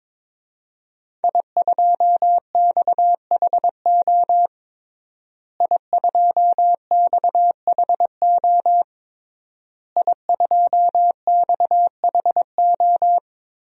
Morse code, heard as a keyed tone, about 22 wpm, 700 Hz. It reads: I2XHO I2XHO I2XHO
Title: DXCC-100 Call Signs - 5 Characters - Repeated 3x - 22wpm